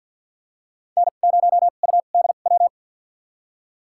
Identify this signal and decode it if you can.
Morse code, heard as a keyed tone, about 52 wpm, spaced as for 37 wpm, 700 Hz. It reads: N0UDW
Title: USA Call Signs - 5 Characters - 2x Character Spacing - 50wpm